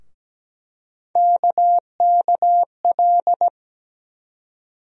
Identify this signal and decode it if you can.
Morse code, heard as a keyed tone, about 17 wpm, 700 Hz. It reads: KKL
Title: USA Call Sign Postfixes - 3 Letters 17wpm